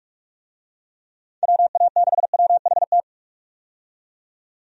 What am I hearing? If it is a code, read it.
Morse code, heard as a keyed tone, about 46 wpm, 700 Hz. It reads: WA6WHT
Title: USA Call Signs - 6 Characters - 45wpm